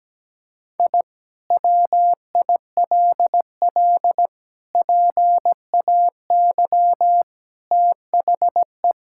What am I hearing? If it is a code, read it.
Morse code, heard as a keyed tone, about 17 wpm, 700 Hz. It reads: I WILL PAY THE
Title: Sentences from Top 1000 Words - Review Part 5 - 17wpm